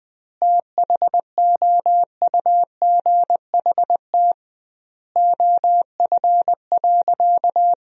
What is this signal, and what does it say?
Morse code, heard as a keyed tone, about 20 wpm, 700 Hz. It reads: THOUGHT OF.
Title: 2x Word-Spacing - Sentences from Top 300 Words - 20wpm